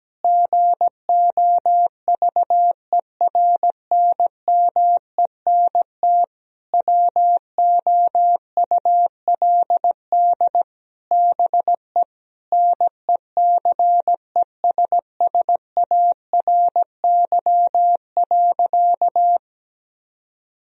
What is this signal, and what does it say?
Morse code, heard as a keyed tone, about 17 wpm, 700 Hz. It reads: GOVERNMENT WOULD BE NECESSARY.
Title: Sentences from Top 1000 Words - Review Part 3 - 17wpm